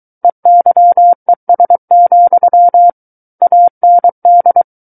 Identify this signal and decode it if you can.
Morse code, heard as a keyed tone, about 23 wpm, 700 Hz. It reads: EYES, AND
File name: Book One / 14-Chapter 14 - 23 WPM